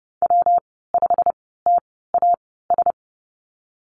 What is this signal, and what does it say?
Morse code, heard as a keyed tone, about 30 wpm, spaced as for 17 wpm, 700 Hz. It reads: W5TAS
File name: Lesson 19g - Letter w - Call Signs 30wpm